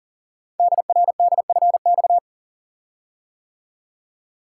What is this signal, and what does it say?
Morse code, heard as a keyed tone, about 41 wpm, 690 Hz. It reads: DRDFX
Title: Code Groups - 5 Letters - 40wpm